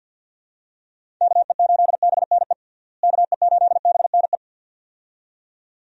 Morse code, heard as a keyed tone, about 49 wpm, 700 Hz. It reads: KE8BNE KE8BNE
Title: USA Call Signs - 6 Characters - Repeated 2x - 50wpm